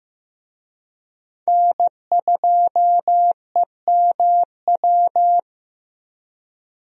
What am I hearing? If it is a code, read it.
Morse code, heard as a keyed tone, about 15 wpm, 700 Hz. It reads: N2EMW
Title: USA Call Signs - 5 Characters - Repeated 2x - 15wpm